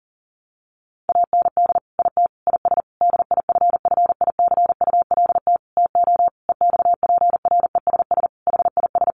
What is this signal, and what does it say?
Morse code, heard as a keyed tone, about 40 wpm, 700 Hz. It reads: AND IT IS DIFFICULT TO EXPRESS HIS
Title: Sentences from Top 1100 Words 40wpm